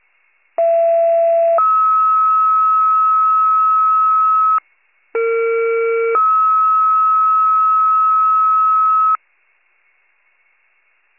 Two-tone page, heard as a one-second tone, then a three-second tone, about 670 then 1290 Hz, about 460 then 1290 Hz. It is heard from 0.6 to 9.2 s.